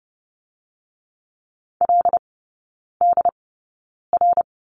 Morse code, heard as a keyed tone, about 32 wpm, spaced as for 9 wpm, 700 Hz. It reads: LDR
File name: Lesson 17e - Letter u - Three Characters 30wpm